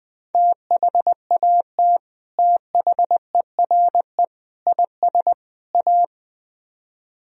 Morse code, heard as a keyed tone, about 20 wpm, 700 Hz. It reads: THAT THERE IS A
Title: Speed-Racing - Sets of 4 Words 17wpm